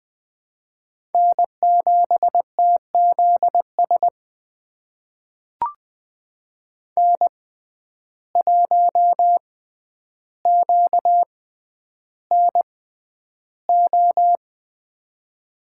Morse code, heard as a keyed tone, about 20 wpm, spaced as for 10 wpm, 700 Hz. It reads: N7TZS N1QNO
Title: USA Call Signs - 5 Characters - 6x Character Spacing - 20wpm